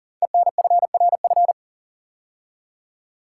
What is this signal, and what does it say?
Morse code, heard as a keyed tone, about 38 wpm, 700 Hz. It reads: ENFRF
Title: Code Groups - 5 Letters - 40wpm